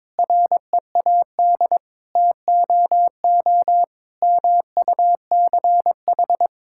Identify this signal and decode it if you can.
Morse code, heard as a keyed tone, about 22 wpm, 700 Hz. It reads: READ TOO MUCH
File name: Sentences with 9-Word Phrases - 22wpm